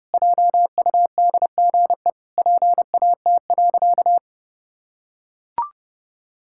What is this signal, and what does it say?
Morse code, heard as a keyed tone, about 30 wpm, 700 Hz. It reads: JUDGE PAT.
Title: Pangram Challenge v2 - 30wpm